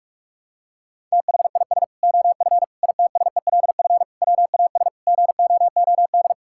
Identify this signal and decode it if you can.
Morse code, heard as a keyed tone, about 46 wpm, 700 Hz. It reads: THIS OF ITSELF WAS GOOD
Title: Sentences from Top 1000 Words - Review Part 5 - 45wpm